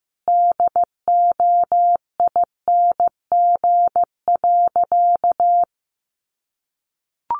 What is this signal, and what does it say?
Morse code, heard as a keyed tone, about 15 wpm, 700 Hz. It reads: DOING.
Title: Sentences from Top 1400 Words 15wpm